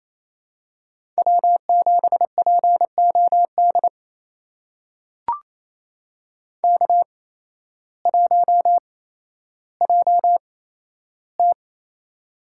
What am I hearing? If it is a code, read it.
Morse code, heard as a keyed tone, about 28 wpm, spaced as for 13 wpm, 700 Hz. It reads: W7POD K1JT
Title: USA Call Signs - 8x Character Spacing - 28wpm